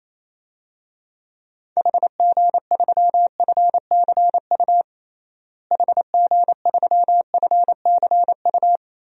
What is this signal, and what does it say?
Morse code, heard as a keyed tone, about 28 wpm, 700 Hz. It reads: HG3FCU HG3FCU
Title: DXCC-100 Call Signs - 6 Characters - Repeated 3x - 28wpm